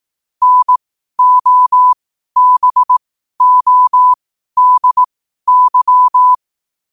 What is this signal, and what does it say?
Morse code, heard as a keyed tone, about 18 wpm, spaced as for 12 wpm, 1000 Hz. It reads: NOBODY
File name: qotdfiles / 12wpm